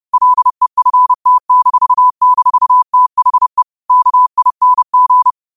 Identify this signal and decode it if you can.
Morse code, heard as a keyed tone, about 30 wpm, 1000 Hz. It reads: LEFT--THE KING
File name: qotdfiles / 30wpm